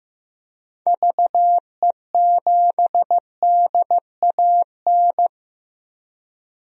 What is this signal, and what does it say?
Morse code, heard as a keyed tone, about 15 wpm, 700 Hz. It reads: VE7DAN
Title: DXCC-100 Call Signs - 6 Characters - Repeated 2x - 15wpm